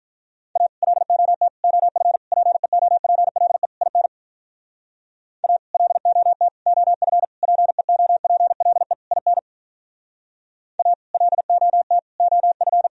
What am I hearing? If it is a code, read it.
Morse code, heard as a keyed tone, about 44 wpm, 700 Hz. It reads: A LOT OF PEOPLE IN A LOT OF PEOPLE IN A LOT OF